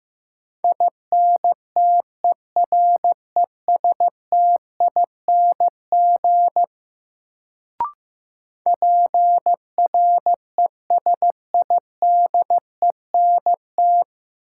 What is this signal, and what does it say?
Morse code, heard as a keyed tone, about 15 wpm, 700 Hz. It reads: INTERESTING PRESIDENT
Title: Sentences from Top 1000 Words - New Word Drill - 15wpm